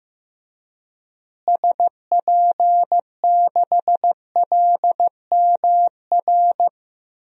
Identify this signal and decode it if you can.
Morse code, heard as a keyed tone, about 15 wpm, 700 Hz. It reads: SP6LMR
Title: DXCC-100 Call Signs - 6 Characters - Repeated 2x - 15wpm